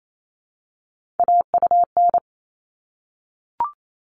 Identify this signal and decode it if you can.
Morse code, heard as a keyed tone, about 28 wpm, 700 Hz. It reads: AUN
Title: All Cut Numbers - 3-Digit 28wpm